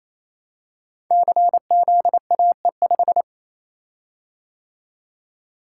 Morse code, heard as a keyed tone, about 27 wpm, 700 Hz. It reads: CZAE5
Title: Code Groups - 5 Letter:Numbers - 28wpm